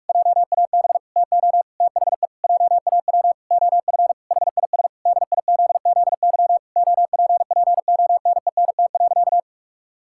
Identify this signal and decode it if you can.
Morse code, heard as a keyed tone, about 41 wpm, 700 Hz. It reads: JAB TO THE JAW OF HIS DIZZY OPPONENT.
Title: Pangram Challenge v2 - 45wpm